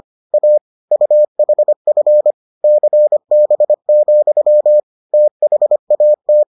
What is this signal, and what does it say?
Morse code, heard as a keyed tone, about 25 wpm, 600 Hz. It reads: A UHF CB, THAT